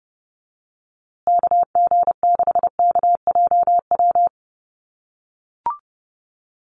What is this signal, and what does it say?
Morse code, heard as a keyed tone, about 29 wpm, 700 Hz. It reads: KG6KJW
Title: USA Call Signs Repeated 2x 30wpm